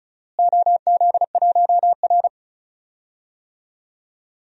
Morse code, heard as a keyed tone, about 36 wpm, 700 Hz. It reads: OZ1R
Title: DXCC-100 Call Signs - 4 Characters - Repeated 3x - 35wpm